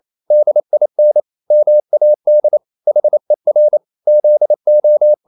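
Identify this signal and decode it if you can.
Morse code, heard as a keyed tone, about 28 wpm, 600 Hz. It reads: DIN MAD HER ZO
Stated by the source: Danish short words sent in Morse at 28 wpm, from CW med Gnister.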